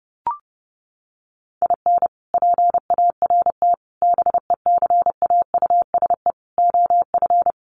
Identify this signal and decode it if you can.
Morse code, heard as a keyed tone, about 30 wpm, 700 Hz. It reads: IN PART BECAUSE OF
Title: Sets of 5 Words Encore2 30wpm